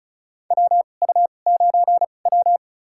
Morse code, heard as a keyed tone, about 36 wpm, spaced as for 25 wpm, 700 Hz. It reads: WU9W
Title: USA Call Signs - 4 Characters - 2x Character Spacing - 35wpm